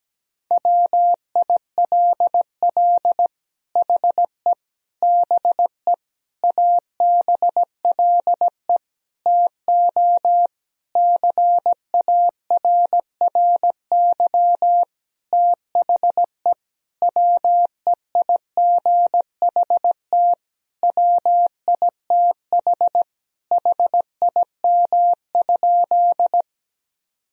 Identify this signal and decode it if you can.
Morse code, heard as a keyed tone, about 17 wpm, 700 Hz. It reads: WILL HE BE ABLE TO CARRY THE WEIGHT WITH HIM?
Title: Sentences from Top 1000 Words - Review Part 3 - 17wpm